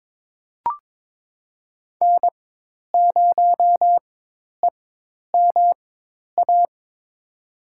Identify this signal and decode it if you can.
Morse code, heard as a keyed tone, about 21 wpm, spaced as for 10 wpm, 700 Hz. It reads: N0EMA
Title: USA Call Signs - 5 Characters - 4x Character Spacing - 22wpm